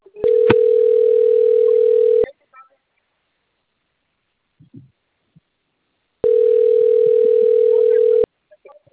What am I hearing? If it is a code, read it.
Ringback tone (North America), 440 + 480 Hz.
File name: out- - - - - .28239